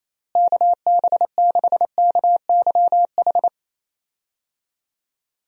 Morse code, heard as a keyed tone, about 28 wpm, 700 Hz. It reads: KB6KYH
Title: USA Call Signs - 6 Characters - Repeated 2x - 28wpm